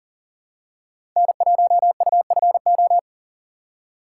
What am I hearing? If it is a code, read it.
Morse code, heard as a keyed tone, about 40 wpm, 700 Hz. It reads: N1UFO
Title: USA Call Signs - 5 Characters - 3x Character Spacing - 40wpm